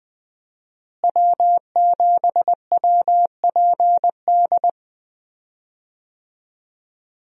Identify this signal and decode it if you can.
Morse code, heard as a keyed tone, about 21 wpm, 700 Hz. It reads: W7WPD
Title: USA Call Signs - 5 Characters - Repeated 2x - 20wpm